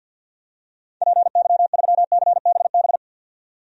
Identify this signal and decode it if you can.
Morse code, heard as a keyed tone, about 52 wpm, 700 Hz. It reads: PY3CBB
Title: DXCC-100 Call Signs - 6 Characters - 50wpm